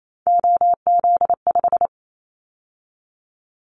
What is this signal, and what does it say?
Morse code, heard as a keyed tone, about 28 wpm, 700 Hz. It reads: OZ5
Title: Callsign Prefixes for Top 50 DXCC - 28wpm